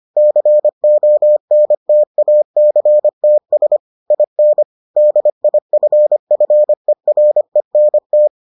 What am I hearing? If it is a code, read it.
Morse code, heard as a keyed tone, about 25 wpm, 600 Hz. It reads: CONTACTS IN DIFFERENT